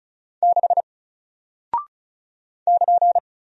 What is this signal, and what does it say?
Morse code, heard as a keyed tone, about 36 wpm, 700 Hz. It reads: 6 <KN>
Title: Single Character (Letter:Number:Punctuation:Prosign) Alt 35wpm